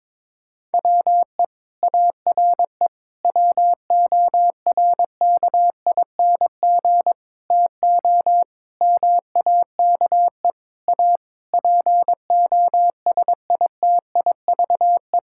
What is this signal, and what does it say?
Morse code, heard as a keyed tone, about 22 wpm, 700 Hz. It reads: WE ARE WORKING TO MAKE A POSITIVE
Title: Sentences from Top 1000 Words - Review Part 6 - 22wpm